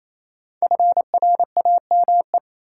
Morse code, heard as a keyed tone, about 28 wpm, 700 Hz. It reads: FRAME
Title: Sentences from Top 1000 Words - Review Part 6 - New Word Drill - 28wpm